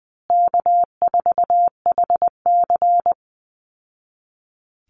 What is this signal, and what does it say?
Morse code, heard as a keyed tone, about 20 wpm, 700 Hz. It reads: K4HC
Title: Easy Call Signs Repeated 2x - 20wpm